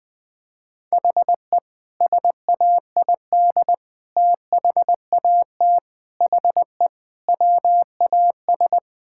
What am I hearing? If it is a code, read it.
Morse code, heard as a keyed tone, about 21 wpm, 700 Hz. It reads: HE SAID THAT HE WAS
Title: Sentences with 5-Word Phrases - 20wpm